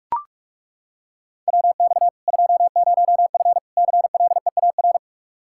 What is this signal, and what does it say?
Morse code, heard as a keyed tone, about 41 wpm, 700 Hz. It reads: WX 20F CLEAR
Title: K1USN SST QSO Practice 45wpm